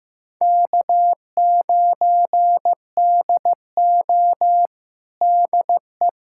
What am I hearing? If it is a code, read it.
Morse code, heard as a keyed tone, about 15 wpm, 700 Hz. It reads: K9DO DE
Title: USA Callsign DE Callsign - 4 Characters - 15wpm